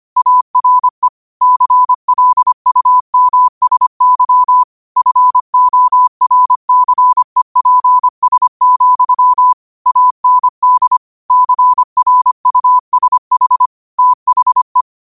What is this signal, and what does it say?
Morse code, heard as a keyed tone, about 25 wpm, 1000 Hz. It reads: ARE CLUMSY FORCEPS, AND CRUSH THE